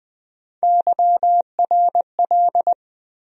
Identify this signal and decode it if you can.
Morse code, heard as a keyed tone, about 21 wpm, 700 Hz. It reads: YRL
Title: USA Call Sign Postfixes - 3 Letters 20wpm